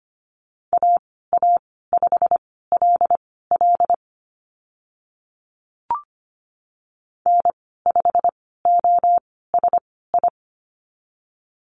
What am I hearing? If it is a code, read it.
Morse code, heard as a keyed tone, about 25 wpm, spaced as for 16 wpm, 690 Hz. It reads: AA5LL N5OSI
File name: Lesson 15g - Number 5 - Call Signs 25wpm